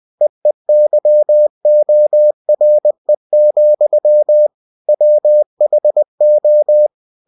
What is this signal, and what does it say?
Morse code, heard as a keyed tone, about 20 wpm, 600 Hz. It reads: EEYORE, WHO
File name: cw-books / winnie-the-pooh / winnie-20wpm-page-0029